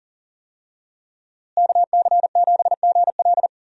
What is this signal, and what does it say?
Morse code, heard as a keyed tone, about 40 wpm, 700 Hz. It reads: KC7GL
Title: USA Call Signs - 5 Characters - Repeated 2x - 40wpm